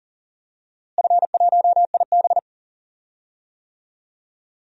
Morse code, heard as a keyed tone, about 40 wpm, 700 Hz. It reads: F1IB
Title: DXCC-100 Call Signs - 4 Characters - Repeated 3x - 40wpm